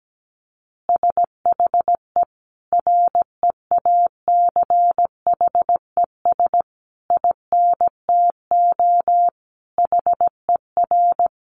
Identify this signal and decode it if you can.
Morse code, heard as a keyed tone, about 17 wpm, 700 Hz. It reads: SHE REACHES INTO HER
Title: Additional Easy Phrases 17wpm